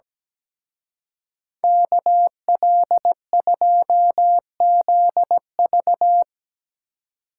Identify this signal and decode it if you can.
Morse code, heard as a keyed tone, about 18 wpm, 700 Hz. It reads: KL2ZV
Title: Speed Racing - USA Call Signs 17wpm